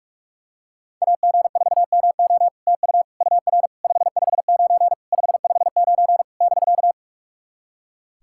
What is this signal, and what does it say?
Morse code, heard as a keyed tone, about 46 wpm, 700 Hz. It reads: AG4MO TU UR 559 559 <BK>
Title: SOTA QSO Practice 45wpm